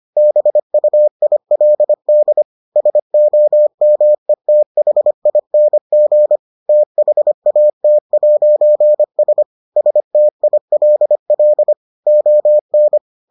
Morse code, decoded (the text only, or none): BUILD SOMETHING THAT'S STILL ON